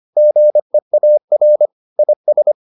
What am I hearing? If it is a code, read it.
Morse code, heard as a keyed tone, about 25 wpm, 600 Hz. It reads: GEAR IS